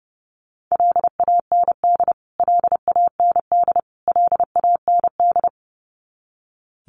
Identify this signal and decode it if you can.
Morse code, heard as a keyed tone, about 30 wpm, 700 Hz. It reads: LAND LAND LAND
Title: Top 300 Words Repeated 30wpm